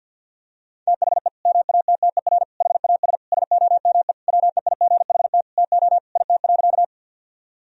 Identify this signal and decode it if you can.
Morse code, heard as a keyed tone, about 51 wpm, 700 Hz. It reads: THE MATTER HAS SOME WEIGHT TO IT.
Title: Sentences from Top 1000 Words - Review Part 3 - 50wpm